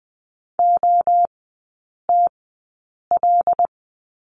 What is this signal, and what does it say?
Morse code, heard as a keyed tone, about 20 wpm, spaced as for 8 wpm, 700 Hz. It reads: OTL
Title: 074 Lesson 13e - Letter L - Three Characters 20wpm